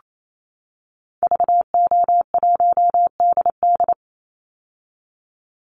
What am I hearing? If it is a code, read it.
Morse code, heard as a keyed tone, about 28 wpm, 700 Hz. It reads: VO1DD
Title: Call Signs - 28wpm